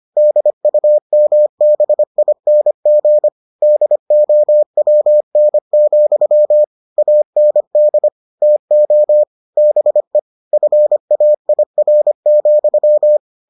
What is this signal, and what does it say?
Morse code, heard as a keyed tone, about 25 wpm, 600 Hz. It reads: DUMBING DOWN, AND TO BE FAIR,